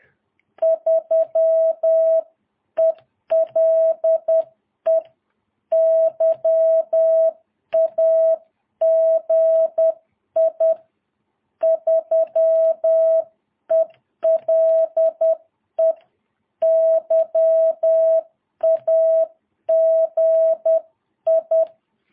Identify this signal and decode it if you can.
Morse code, heard as a keyed tone, about 10 wpm, 650 Hz. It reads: 3ELE YAGI 3ELE YAGI
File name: jh8cbh / 2022cw